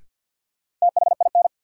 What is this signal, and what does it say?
Morse code, heard as a keyed tone, about 52 wpm, 700 Hz. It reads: THIN
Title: Sentences from Top 1200 Words - New Word Drill - 50wpm